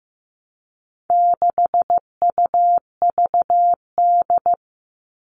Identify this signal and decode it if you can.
Morse code, heard as a keyed tone, about 15 wpm, 700 Hz. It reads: 6UVD
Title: All Cut Numbers - 4-Digit 15wpm